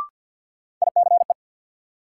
Morse code, heard as a keyed tone, about 48 wpm, 700 Hz. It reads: ICE
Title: Sentences from Top 1000 Words - New Word Drill - 50wpm